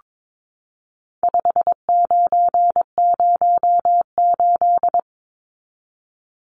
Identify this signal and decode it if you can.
Morse code, heard as a keyed tone, about 22 wpm, 700 Hz. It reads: 5908
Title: Numbers - 4-Digit 22wpm